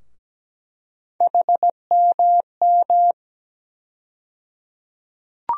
Morse code, heard as a keyed tone, about 17 wpm, 700 Hz. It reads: HMM
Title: USA Call Sign Postfixes - 3 Letters 17wpm